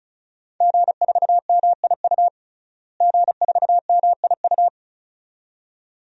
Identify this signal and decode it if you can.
Morse code, heard as a keyed tone, about 34 wpm, 700 Hz. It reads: G4MIU G4MIU
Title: DXCC-100 Call Signs - 5 Characters - Repeated 2x - 35wpm